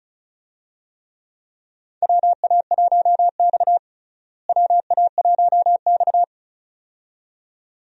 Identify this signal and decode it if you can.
Morse code, heard as a keyed tone, about 36 wpm, 700 Hz. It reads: WA1X WA1X
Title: USA Call Signs - 4 Characters - Repeated 2x - 35wpm